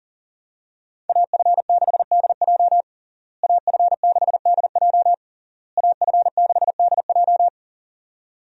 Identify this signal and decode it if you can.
Morse code, heard as a keyed tone, about 40 wpm, 700 Hz. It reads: AF6DJ AF6DJ AF6DJ
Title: USA Call Signs - 5 Characters - Repeated 3x - 40wpm